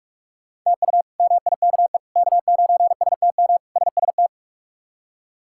Morse code, heard as a keyed tone, about 46 wpm, 700 Hz. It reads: TU MIKE K9STM SST
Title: K1USN SST QSO Practice - 45wpm